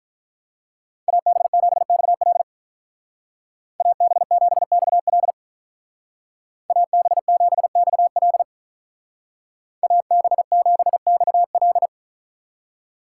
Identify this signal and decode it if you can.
Morse code, heard as a keyed tone, about 47 wpm, 700 Hz. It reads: AB7XL AB7XL AB7XL AB7XL